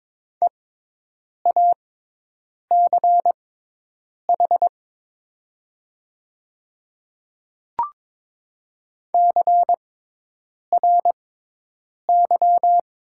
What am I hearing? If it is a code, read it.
Morse code, heard as a keyed tone, about 22 wpm, spaced as for 8 wpm, 700 Hz. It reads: EACH CRY
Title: 6x Character Spacing - Top 500 Words - 22wpm